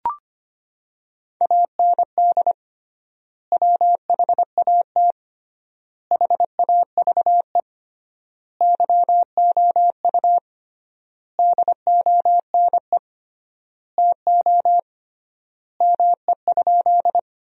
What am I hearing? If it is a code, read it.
Morse code, heard as a keyed tone, about 25 wpm, 700 Hz. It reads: AND WHAT HAVE YOU DONE TO ME?
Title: 3x Word-Spacing - Sentences from Top 500 Words - 25wpm